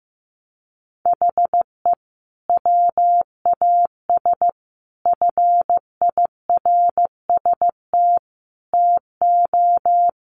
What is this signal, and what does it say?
Morse code, heard as a keyed tone, about 15 wpm, 700 Hz. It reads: HE WAS FIRST TO
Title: Sentences from Top 1100 Words 15wpm